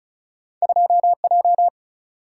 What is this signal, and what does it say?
Morse code, heard as a keyed tone, about 36 wpm, 700 Hz. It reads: 2J
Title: Single-Letter Single-Number 35wpm